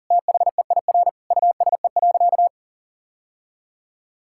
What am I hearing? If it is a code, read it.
Morse code, heard as a keyed tone, about 40 wpm, 700 Hz. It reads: THEIR USE.
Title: Sentences from Top 1000 Words - Review Part 6 - 40wpm